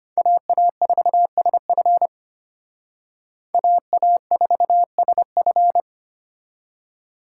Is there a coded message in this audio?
Yes: Morse code from 0.2 to 5.8 s, heard as a keyed tone, about 33 wpm, 700 Hz. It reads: AA4SF AA4SF